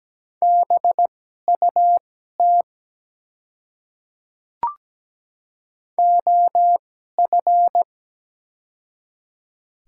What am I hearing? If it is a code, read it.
Morse code, heard as a keyed tone, about 17 wpm, spaced as for 12 wpm, 700 Hz. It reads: BUT OF
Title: 2x Character Spacing - Top 200 Words - 17wpm